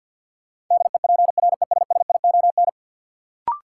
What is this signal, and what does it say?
Morse code, heard as a keyed tone, about 53 wpm, 700 Hz. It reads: DEPRESSION
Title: Sentences from Top 1300 Words - New Word Drill Alt - 50wpm